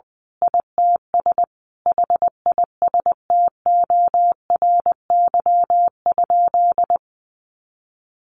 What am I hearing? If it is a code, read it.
Morse code, heard as a keyed tone, about 20 wpm, 700 Hz. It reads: ITS HISTORY?
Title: Sentences from Top 1100 Words 20wpm